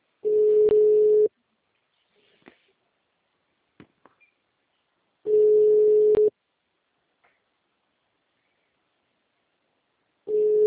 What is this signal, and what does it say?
Ringback tone (Europe, 425 Hz).